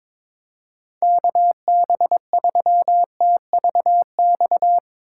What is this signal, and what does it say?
Morse code, heard as a keyed tone, about 21 wpm, 700 Hz. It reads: KB3TVX
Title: USA Call Signs - 6 Characters - 4x Character Spacing - 22wpm